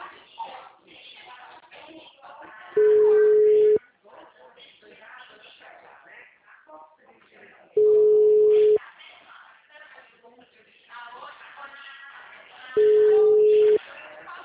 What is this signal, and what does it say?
Ringback tone (Europe, 425 Hz), with speech.